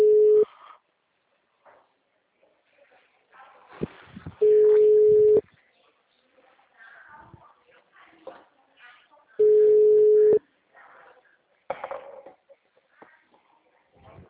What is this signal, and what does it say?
Ringback tone (Europe, 425 Hz), with speech.